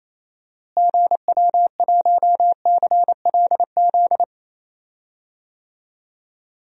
Morse code, heard as a keyed tone, about 28 wpm, 700 Hz. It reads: GW1CLZ
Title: DXCC-100 Call Signs - 6 Characters - Repeated 3x - 28wpm